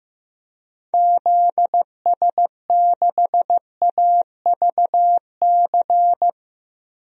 Morse code, heard as a keyed tone, about 16 wpm, 700 Hz. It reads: ZS6AVC
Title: DXCC-100 Call Signs - 6 Characters - Repeated 3x - 15wpm